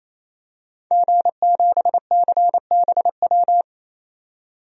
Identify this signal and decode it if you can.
Morse code, heard as a keyed tone, about 28 wpm, 700 Hz. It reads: G7CBW
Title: DXCC-100 Call Signs - 5 Characters - 28wpm